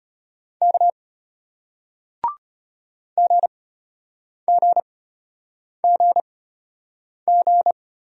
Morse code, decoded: K GGGG

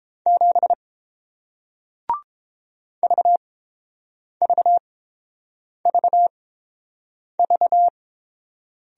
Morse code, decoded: 7 VVVV

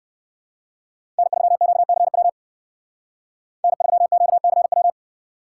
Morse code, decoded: N3QCW N3QCW

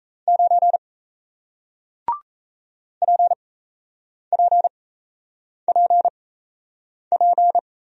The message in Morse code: 9 PPPP

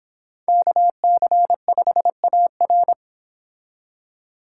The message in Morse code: KC5AR

